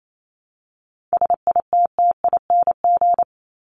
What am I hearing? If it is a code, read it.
Morse code, heard as a keyed tone, about 28 wpm, 700 Hz. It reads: SITTING
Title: ING Suffix - Top 100 Words with Base Words 28wpm